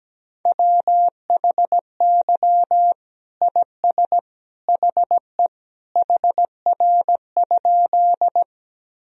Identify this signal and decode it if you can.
Morse code, heard as a keyed tone, about 17 wpm, 700 Hz. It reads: WHY IS HE HR?
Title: Sentences from Top 200 Words and Abbreviations 17wpm